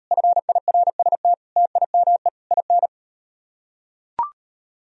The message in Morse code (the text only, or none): FIRST TIME IN